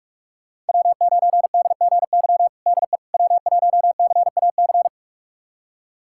Morse code, heard as a keyed tone, about 43 wpm, 700 Hz. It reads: W9DGY DE W1CAC
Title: USA Callsign DE Callsign - 5 Characters - 45wpm